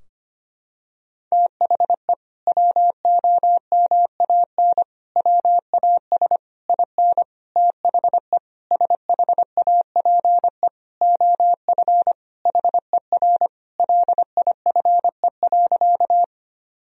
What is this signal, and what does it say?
Morse code, heard as a keyed tone, about 25 wpm, 700 Hz. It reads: THE WOMAN WAS IN THE SHAPE OF HER LIFE.
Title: Sentences from Top 1000 Words - Review Part 3 - 25wpm